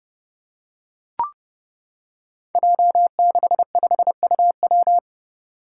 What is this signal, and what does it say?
Morse code, heard as a keyed tone, about 29 wpm, 700 Hz. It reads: J65UW
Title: Five Challenging Letter-Number Combinations 30wpm